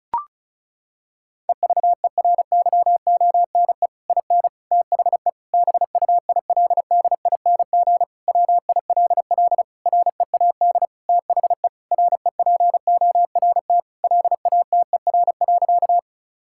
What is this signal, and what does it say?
Morse code, heard as a keyed tone, about 36 wpm, 700 Hz. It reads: EVERYONE IN THE BUILDING WILL READ THE REPORT LATER.
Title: Sentences from Top 1000 Words - Review Part 6 - 35wpm